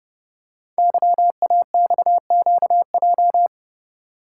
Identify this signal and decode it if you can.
Morse code, heard as a keyed tone, about 29 wpm, 700 Hz. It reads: YAXQJ
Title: Code Groups - 5 Letters - 30wpm